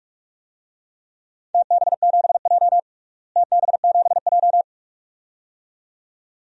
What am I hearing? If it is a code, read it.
Morse code, heard as a keyed tone, about 45 wpm, 700 Hz. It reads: TB7J TB7J
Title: DXCC-100 Call Signs - 4 Characters - Repeated 2x - 45wpm